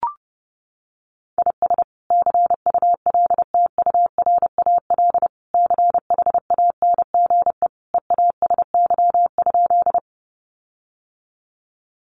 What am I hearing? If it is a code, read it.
Morse code, heard as a keyed tone, about 28 wpm, 700 Hz. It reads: IS CULTURAL CHANGE EASY?
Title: Sentences from Top 1500 Words 30wpm